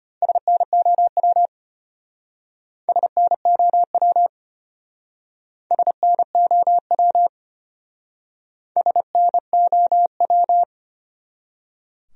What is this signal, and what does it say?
Morse code, heard as a keyed tone, about 33 wpm, 700 Hz. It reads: SNOW SNOW SNOW SNOW